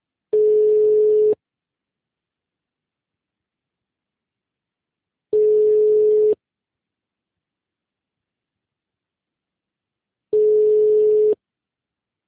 Ringback tone (Europe, 425 Hz).
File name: out- -2043- - - .37